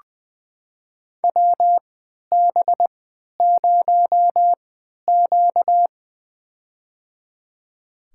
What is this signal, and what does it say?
Morse code, heard as a keyed tone, about 20 wpm, spaced as for 11 wpm, 700 Hz. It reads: WB0Q
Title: USA Call Signs - 4 Characters - 3x Character Spacing - 20wpm